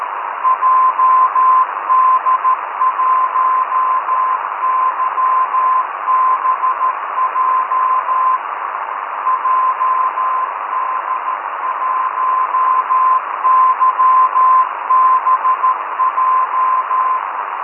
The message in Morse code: JD1YBJ JD1YBJ